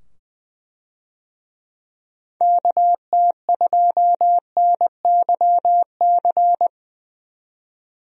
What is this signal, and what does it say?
Morse code, heard as a keyed tone, about 20 wpm, 700 Hz. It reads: KT2NYC
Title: USA Call Signs - 6 Characters - 20wpm